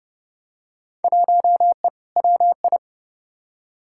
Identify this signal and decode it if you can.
Morse code, heard as a keyed tone, about 32 wpm, 700 Hz. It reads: 1E WI
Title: Field Day Exchanges - 30wpm